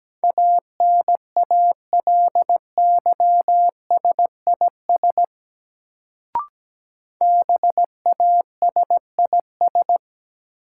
Morse code, heard as a keyed tone, about 17 wpm, 700 Hz. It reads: ANALYSIS BASIS
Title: Sentences from Top 1000 Words - Review Part 5 - New Word Drill - 17wpm